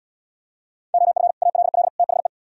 Morse code, heard as a keyed tone, about 77 wpm, 700 Hz. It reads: OF ALL THE